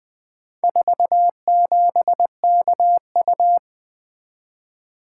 Morse code, heard as a keyed tone, about 20 wpm, 700 Hz. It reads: 47KU
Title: Four Letter-Number Combinations - 20wpm